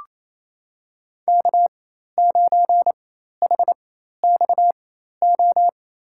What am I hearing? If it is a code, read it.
Morse code, heard as a keyed tone, about 28 wpm, spaced as for 13 wpm, 700 Hz. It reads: K9HXO
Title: USA Call Signs - 5 Characters - 4x Character Spacing - 28wpm